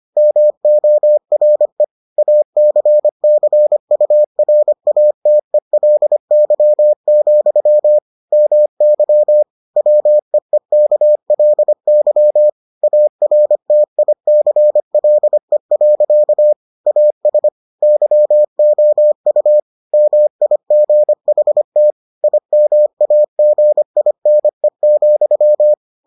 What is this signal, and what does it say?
Morse code, heard as a keyed tone, about 25 wpm, 600 Hz. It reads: MORE ACCURATELY, MY WEEKLY ARTICLE. AS YOU MIGHT IMAGINE,